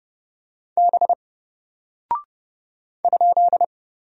Morse code, heard as a keyed tone, about 30 wpm, 700 Hz. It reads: B ?
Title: Single Character (Letter:Number:Punctuation:Prosign) Alt 30wpm